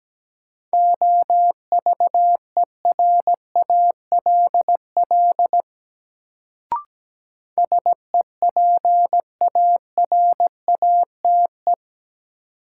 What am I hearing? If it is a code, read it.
Morse code, heard as a keyed tone, about 17 wpm, 700 Hz. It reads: OVERALL SEPARATE
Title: Sentences from Top 1000 Words - Review Part 7 - New Word Drill - 17wpm